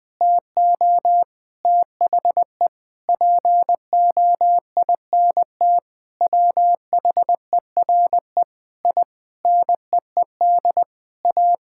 Morse code, heard as a keyed tone, about 20 wpm, 700 Hz. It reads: TO THE POINT WHERE I NEED A
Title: Sentences with 4-Word Phrases - 20wpm